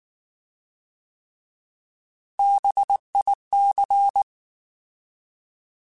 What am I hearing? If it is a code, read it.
Morse code, heard as a keyed tone, about 19 wpm, 800 Hz. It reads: BIC
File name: Alphabetical / Alphabetical mp3s / Bs @18